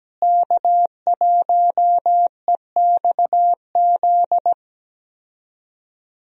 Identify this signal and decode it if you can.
Morse code, heard as a keyed tone, about 17 wpm, 700 Hz. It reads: K1EXZ
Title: USA Call Signs - 5 Characters - Repeated 3x - 17wpm